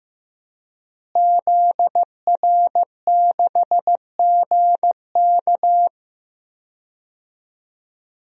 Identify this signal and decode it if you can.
Morse code, heard as a keyed tone, about 16 wpm, 700 Hz. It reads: ZR6GK
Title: DXCC-100 Call Signs - 5 Characters - Repeated 3x - 15wpm